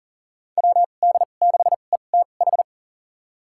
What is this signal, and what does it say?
Morse code, heard as a keyed tone, about 40 wpm, spaced as for 29 wpm, 700 Hz. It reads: WD6ETH
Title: USA Call Signs - 6 Characters - 2x Character Spacing - 40wpm